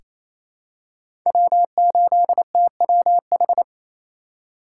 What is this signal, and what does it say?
Morse code, heard as a keyed tone, about 28 wpm, 700 Hz. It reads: W8TWH
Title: USA Call Signs - 5 Characters - 2x Character Spacing - 28wpm